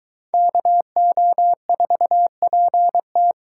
Morse code, heard as a keyed tone, about 21 wpm, 700 Hz. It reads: KO4PT